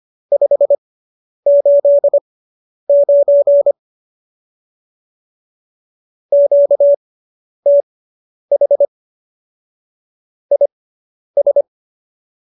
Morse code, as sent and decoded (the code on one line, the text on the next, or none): ..... ---.. ----. / --.- - .... / .. ...
589 QTH IS